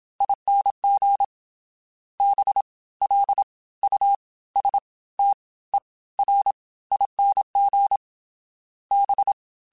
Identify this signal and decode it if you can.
Morse code, heard as a keyed tone, about 27 wpm, spaced as for 18 wpm, 800 Hz. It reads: ING BLUSTERING B
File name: Special INGs / Special INGs-mp3s / Fast ing @25